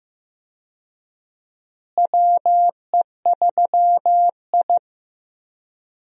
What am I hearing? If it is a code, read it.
Morse code, heard as a keyed tone, about 16 wpm, 700 Hz. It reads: WE3I
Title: USA Call Signs - 4 Characters - Repeated 3x - 15wpm